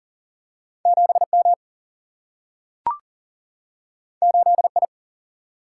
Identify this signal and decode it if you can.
Morse code, heard as a keyed tone, about 40 wpm, 700 Hz. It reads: 7 M 8 I
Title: Single-Letter Single-Number 40wpm